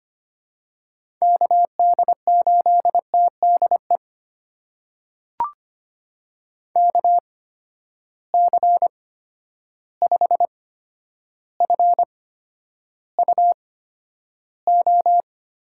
Morse code, heard as a keyed tone, about 25 wpm, spaced as for 11 wpm, 700 Hz. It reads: KD8TDE KC5FUO
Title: USA Call Signs - 6 Characters - 8x Character Spacing - 25wpm